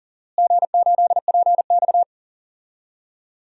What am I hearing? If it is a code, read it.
Morse code, heard as a keyed tone, about 39 wpm, 700 Hz. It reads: G8PX
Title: DXCC-100 Call Signs - 4 Characters - Repeated 3x - 40wpm